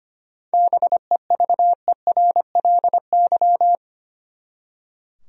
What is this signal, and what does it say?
Morse code, heard as a keyed tone, about 25 wpm, 700 Hz. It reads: BEVERLY
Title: USA Common Names - 25wpm